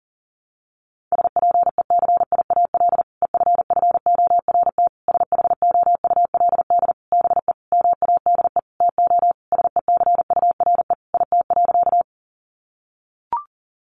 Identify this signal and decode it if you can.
Morse code, heard as a keyed tone, about 40 wpm, 700 Hz. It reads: SPECIAL EFFORT SHOULD BE MADE TO SECURE IT.
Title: Sentences from Top 1200 Words 40wpm